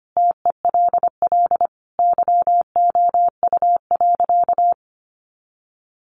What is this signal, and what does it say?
Morse code, heard as a keyed tone, about 25 wpm, 700 Hz. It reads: TELL YOU.
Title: Sentences from Top 1100 Words 25wpm